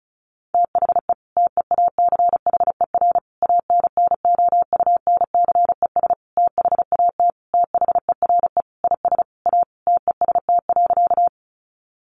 Morse code, heard as a keyed tone, about 34 wpm, 700 Hz. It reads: THE TEACHER ANNOUNCES THAT THERE IS A TEST.
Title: Additional Easy Phrases 35wpm